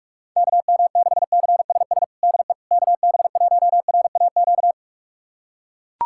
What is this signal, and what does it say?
Morse code, heard as a keyed tone, about 42 wpm, 700 Hz. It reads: KM6CSS DE KB1RAQ
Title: USA Callsign DE Callsign 45wpm